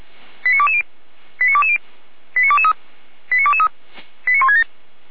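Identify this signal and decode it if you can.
Five-tone selective call (ZVEI), heard as quick runs of five tones, tones of about 70 ms. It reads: 89220 89220 89223 89223 89167
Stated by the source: ZVEI five-tone selective call received by SDR, from the Feuerwehr Orlamünde fire brigade audio archive.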